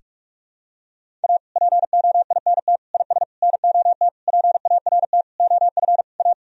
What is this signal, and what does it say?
Morse code, heard as a keyed tone, about 41 wpm, 700 Hz. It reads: A POINT IS NOT PART OF A
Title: Sentences from Top 1000 Words - Review Part 1 - 45wpm